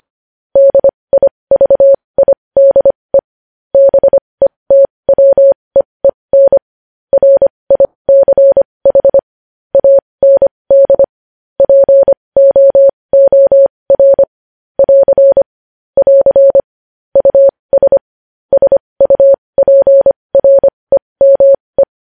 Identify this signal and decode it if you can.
Morse code, heard as a keyed tone, about 25 wpm, spaced as for 20 wpm, 550 Hz. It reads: DIVIDE BETWEEN RICH AND POOR <AR> <AR> US SUPREME
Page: news-headlines-in-morse-code-at-20-wpm